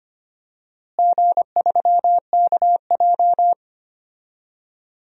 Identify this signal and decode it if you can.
Morse code, heard as a keyed tone, about 25 wpm, 700 Hz. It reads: G3KJ
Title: DXCC-100 Call Signs - 4 Characters - Repeated 2x - 25wpm